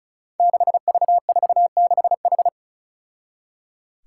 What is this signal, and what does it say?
Morse code, heard as a keyed tone, about 36 wpm, 700 Hz. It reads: 6V46H
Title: Five Challenging Letter-Number Combinations 35wpm